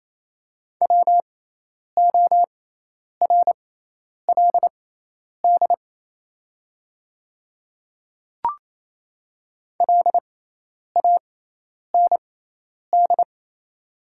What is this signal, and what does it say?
Morse code, heard as a keyed tone, about 28 wpm, spaced as for 10 wpm, 700 Hz. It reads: WORLD LAND
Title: 6x Character Spacing - Top 200 Words - 28wpm